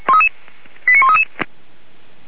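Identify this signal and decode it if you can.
Five-tone selective call (ZVEI), heard as quick runs of five tones, tones of about 70 ms. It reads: 89133
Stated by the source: ZVEI five-tone selective call received by SDR, fire brigade Feuerwehr Orlamünde.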